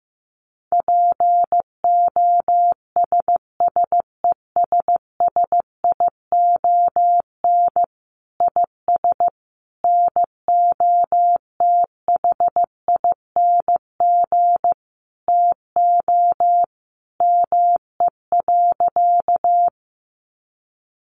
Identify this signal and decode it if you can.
Morse code, heard as a keyed tone, about 15 wpm, 700 Hz. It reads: POSSESSION IS NOTHING TO ME.
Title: Sentences from Top 1400 Words 15wpm